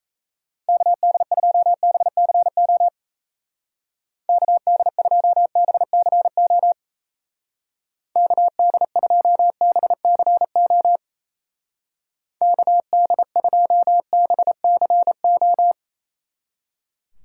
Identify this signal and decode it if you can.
Morse code, heard as a keyed tone, about 34 wpm, 700 Hz. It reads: KD2BCO KD2BCO KD2BCO KD2BCO
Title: Speed Racing - USA 2x3 Call Signs 28wpm